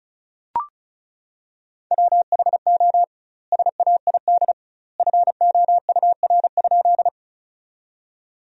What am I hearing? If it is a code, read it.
Morse code, heard as a keyed tone, about 36 wpm, 700 Hz. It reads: WHO SAID FOUR?
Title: 2x Word-Spacing - Sentences from Top 300 Words - 35wpm